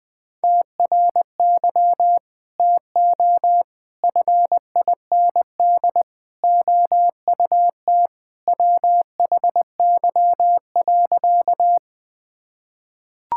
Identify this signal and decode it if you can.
Morse code, heard as a keyed tone, about 21 wpm, 700 Hz. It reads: TRY TO FIND OUT WHY.
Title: Sentences with 3-Word Phrases - Spoken Beginning - 20wpm